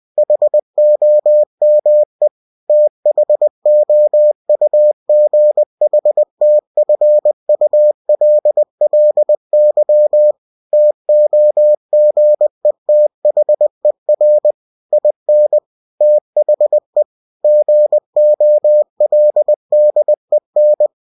Morse code, heard as a keyed tone, about 20 wpm, 600 Hz. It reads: HOME THOUGHTFULLY TOGETHER IN THE GOLDEN
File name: cw-books / winnie-the-pooh / winnie-20wpm-page-0110